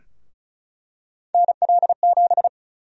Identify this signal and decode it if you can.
Morse code, heard as a keyed tone, about 36 wpm, 700 Hz. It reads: NL7
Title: USA Call Sign Prefixes - Repeated 2x - 35wpm